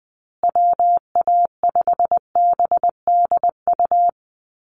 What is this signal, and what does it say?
Morse code, heard as a keyed tone, about 20 wpm, 700 Hz. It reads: WA5BDU
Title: Call Signs Repeated 3x - 20wpm